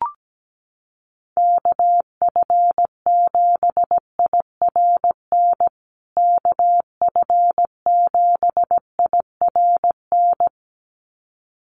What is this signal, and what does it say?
Morse code, heard as a keyed tone, about 17 wpm, 700 Hz. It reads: KF7IRN KF7IRN
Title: USA Call Signs Repeated 2x 17wpm